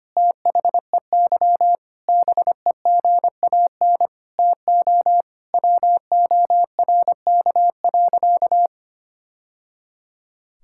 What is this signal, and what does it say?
Morse code, heard as a keyed tone, about 25 wpm, 700 Hz. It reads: THEY BEGAN TO WORK.
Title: Sentences from Top 1000 Words - Review Part 6 - 25wpm